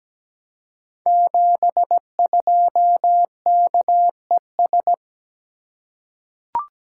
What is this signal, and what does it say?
Morse code, heard as a keyed tone, about 17 wpm, 700 Hz. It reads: 72KES
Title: Code Groups - 5 Letter:Numbers - 17wpm